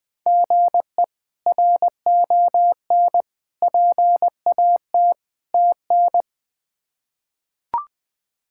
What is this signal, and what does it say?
Morse code, heard as a keyed tone, about 21 wpm, 700 Hz. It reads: GE RON PAT TN
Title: K1USN SST QSO Practice - 20wpm